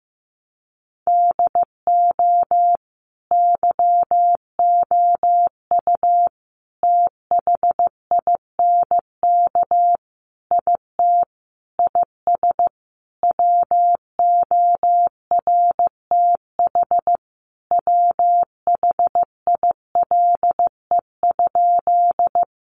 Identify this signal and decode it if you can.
Morse code, heard as a keyed tone, about 15 wpm, 700 Hz. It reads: DO YOU THINK IT IS WORTH WHILE?
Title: Sentences from Top 800 Words 15wpm